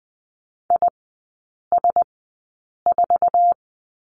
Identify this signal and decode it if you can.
Morse code, heard as a keyed tone, about 20 wpm, spaced as for 8 wpm, 690 Hz. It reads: IS4
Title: 166 Lesson 27e - Number 7 - Three Characters 20wpm